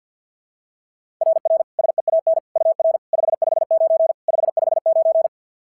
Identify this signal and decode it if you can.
Morse code, heard as a keyed tone, about 50 wpm, 690 Hz. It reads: RR SEAN UR 559 559
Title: SOTA QSO Practice 50wpm